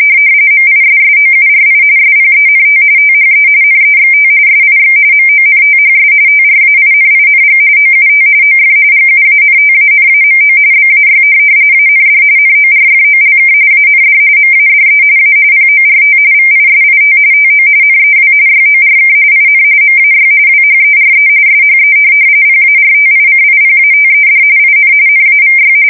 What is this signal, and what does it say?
RTTY, about 45.45 baud. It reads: MISSOURI SPENT THE NEXT SIX MONTHS TRAINING OUT OF GUANTANAMO BAY AND NORFOLK AND MADE A PORT VISIT TO NEW YORK IN MAY WHERE SHE PARTICIPATED IN NAVY
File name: fsk170 / USS-Missouri